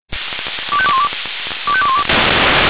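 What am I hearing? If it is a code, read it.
Five-tone selective call (ZVEI), heard as quick runs of five tones, tones of about 70 ms. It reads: 25212 25212
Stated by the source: ZVEI five-tone selective call received on SDR.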